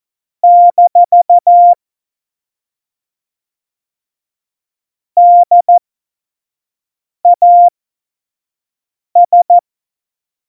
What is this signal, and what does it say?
Morse code, heard as a keyed tone, about 14 wpm, spaced as for 5 wpm, 700 Hz. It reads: - DAS
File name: Morsekurs / Witze / Witz 14. Januar 2020, 25 F70